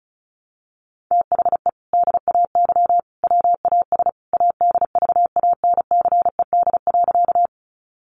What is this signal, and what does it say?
Morse code, heard as a keyed tone, about 34 wpm, 700 Hz. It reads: THE DAY WAS ADVANCED.
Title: Sentences from Top 1200 Words 35wpm